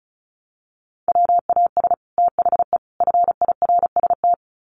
Morse code, heard as a keyed tone, about 34 wpm, 700 Hz. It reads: WAS THE FIRST
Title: Sets of 3 Words 35wpm